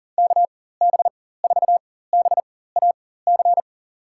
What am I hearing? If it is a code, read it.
Morse code, heard as a keyed tone, about 40 wpm, spaced as for 19 wpm, 700 Hz. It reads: KB4BAC
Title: USA Call Signs - 6 Characters - 4x Character Spacing - 40wpm